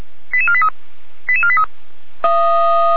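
Five-tone selective call (ZVEI), heard as quick runs of five tones, tones of about 70 ms. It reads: 88483 88483
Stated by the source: ZVEI five-tone selective call 88337 received by SDR.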